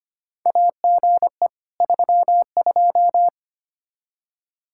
Morse code, heard as a keyed tone, about 25 wpm, 700 Hz. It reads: AGE 32
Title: K1USN SST QSO Practice 25wpm